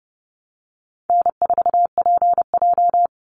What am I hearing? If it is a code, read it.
Morse code, heard as a keyed tone, about 30 wpm, 700 Hz. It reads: N4PJ
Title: Easy Call Signs Repeated 3x - 30wpm